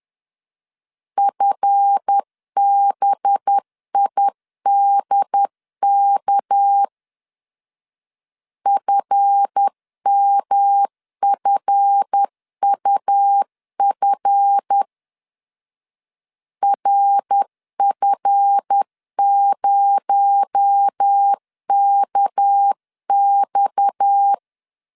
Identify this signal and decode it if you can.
Morse code, heard as a keyed tone, about 10 wpm, 800 Hz. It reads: FBIDK FMFUF RF0KX
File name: LEZIONI / 7 GF9Z / 7 GF 10wpm 2514